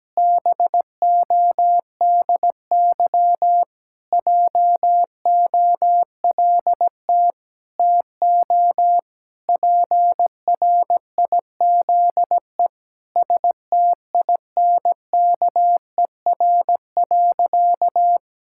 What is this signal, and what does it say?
Morse code, heard as a keyed tone, about 17 wpm, 700 Hz. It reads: BODY JOLT TO PRIZE STINKER.
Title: Pangram Challenge v2 - 17wpm